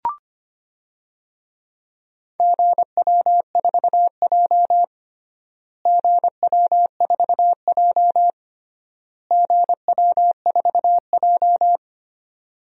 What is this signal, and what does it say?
Morse code, heard as a keyed tone, about 25 wpm, 700 Hz. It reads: GW4J GW4J GW4J
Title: DXCC-100 Call Signs - 4 Characters - Repeated 3x - 25wpm